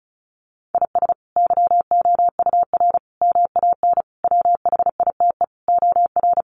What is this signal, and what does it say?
Morse code, heard as a keyed tone, about 34 wpm, 700 Hz. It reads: IS YOUR MAN WHITE OR
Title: Sentences from Top 400 Words 35wpm